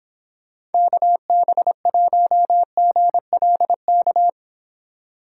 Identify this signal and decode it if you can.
Morse code, heard as a keyed tone, about 29 wpm, 700 Hz. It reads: KB1GLK